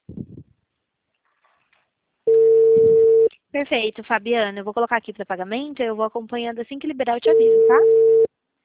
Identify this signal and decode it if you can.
Ringback tone (China, 450 Hz), with speech.
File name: out- -2005- - - .46182